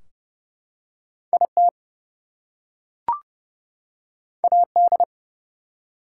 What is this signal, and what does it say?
Morse code, heard as a keyed tone, about 30 wpm, 700 Hz. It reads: IT AD
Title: Two Letter Words - 30wpm